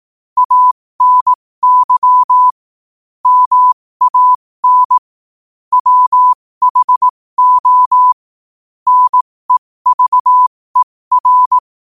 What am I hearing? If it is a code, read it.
Morse code, heard as a keyed tone, about 18 wpm, spaced as for 15 wpm, 1000 Hz. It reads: ANY MAN WHO NEVER
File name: qotdfiles / 15wpm